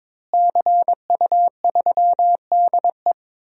Morse code, heard as a keyed tone, about 22 wpm, 700 Hz. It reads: CU3DE
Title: DXCC-100 Call Signs - 5 Characters - Repeated 3x - 22wpm